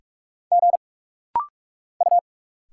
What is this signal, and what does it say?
Morse code, heard as a keyed tone, about 46 wpm, 700 Hz. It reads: G U
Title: Single Letters - Rapid-Fire 45wpm